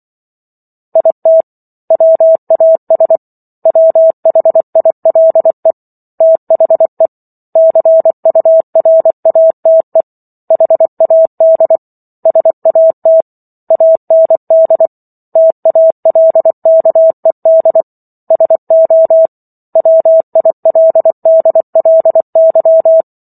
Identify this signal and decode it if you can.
Morse code, heard as a keyed tone, about 24 wpm, 650 Hz. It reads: IT WAS WHILE THE CURATE HAD SAT AND TALKED SO WILDLY